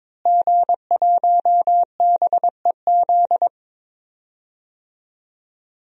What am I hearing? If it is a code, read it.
Morse code, heard as a keyed tone, about 21 wpm, 700 Hz. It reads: G1BEZ
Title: DXCC-100 Call Signs - 5 Characters - Repeated 2x - 22wpm